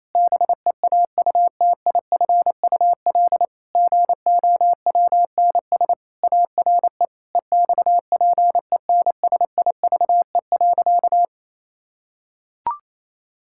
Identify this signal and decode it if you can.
Morse code, heard as a keyed tone, about 28 wpm, 700 Hz. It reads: BEAUTIFUL GOWNS ARE EXPENSIVE.
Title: Pangram Challenge v2 - 28wpm